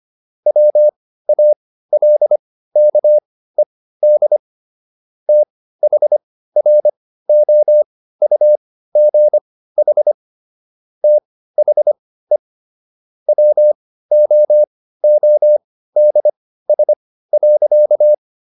Morse code, heard as a keyed tone, about 25 wpm, spaced as for 15 wpm, 600 Hz. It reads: WALKED THROUGH THE WOODS.
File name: SS119 15